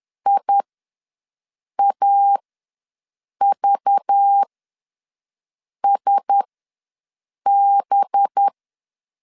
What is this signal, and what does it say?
Morse code, heard as a keyed tone, about 10 wpm, spaced as for 5 wpm, 800 Hz. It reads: IAVSB
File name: LEZIONI / 6 PKX0 / 6 PK ws 10wpm 3745